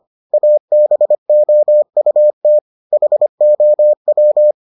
Morse code, heard as a keyed tone, about 25 wpm, 600 Hz. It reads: ABOUT HOW